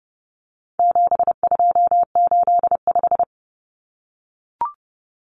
Morse code, heard as a keyed tone, about 30 wpm, 700 Hz. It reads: 7285
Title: Common Cut Numbers - 4-Digit 30wpm